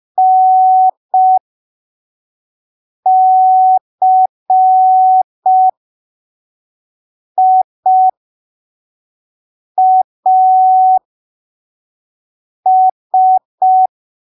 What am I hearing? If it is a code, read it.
Morse code, heard as a keyed tone, about 5 wpm, spaced as for 3 wpm, 750 Hz. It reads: NCIAS
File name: arquivos / texto / texto 06 05ppm